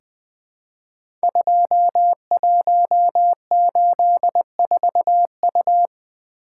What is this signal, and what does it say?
Morse code, heard as a keyed tone, about 20 wpm, 700 Hz. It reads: 2184U
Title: Code Groups - 5 Letter:Numbers - 20wpm